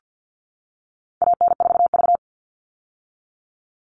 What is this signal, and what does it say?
Morse code, heard as a keyed tone, about 50 wpm, 700 Hz. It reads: AN4V
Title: All Cut Numbers - 4-Digit 50wpm